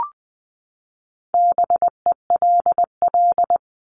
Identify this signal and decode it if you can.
Morse code, heard as a keyed tone, about 20 wpm, 700 Hz. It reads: BELL
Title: Top 1751-2000 Words 20wpm